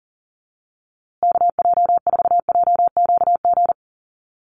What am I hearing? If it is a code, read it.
Morse code, heard as a keyed tone, about 40 wpm, 700 Hz. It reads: KJ4JQG
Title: USA Call Signs Repeated 3x 40wpm